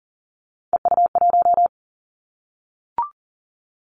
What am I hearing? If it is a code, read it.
Morse code, heard as a keyed tone, about 42 wpm, 700 Hz. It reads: EU1
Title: Callsign Prefixes for Top 50 DXCC - 40wpm